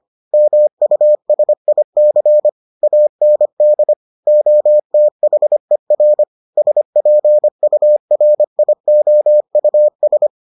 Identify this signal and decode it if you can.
Morse code, heard as a keyed tone, about 25 wpm, 600 Hz. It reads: MUSIC AND OTHER SPURIOUS